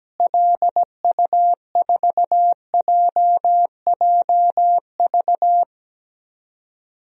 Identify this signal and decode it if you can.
Morse code, heard as a keyed tone, about 17 wpm, 700 Hz. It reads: LU4JJV
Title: DXCC-100 Call Signs - 6 Characters - Repeated 3x - 17wpm